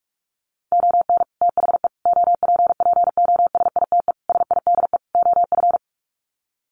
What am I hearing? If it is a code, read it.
Morse code, heard as a keyed tone, about 44 wpm, 700 Hz. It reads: ON THE OPPOSITE SIDE OF
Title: Sets of 5 Words Encore2 45wpm